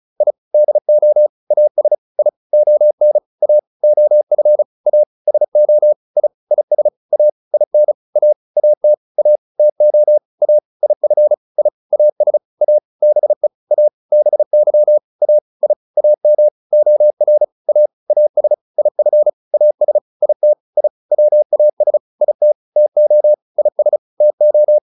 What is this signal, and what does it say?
Morse code, heard as a keyed tone, about 34 wpm, 600 Hz. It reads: I DO AS I ON A OF A SO I IS A IN A AT A TO A IF I AS A BE A BY A I AM OR A AS IF AS IT I WAS IT TO IS TO